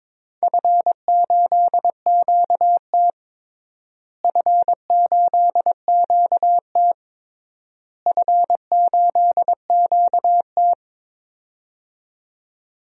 Morse code, heard as a keyed tone, about 21 wpm, 700 Hz. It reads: F8QT F8QT F8QT
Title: DXCC-100 Call Signs - 4 Characters - Repeated 3x - 22wpm